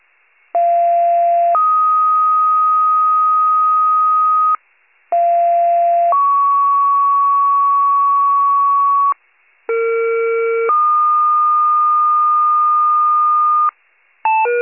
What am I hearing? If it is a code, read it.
Two-tone page, heard as a one-second tone, then a three-second tone, about 690 then 1250 Hz, about 690 then 1120 Hz, about 460 then 1220 Hz. It is heard from 0.5 to 13.7 s.
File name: pager / 21 RESCUE / wcalert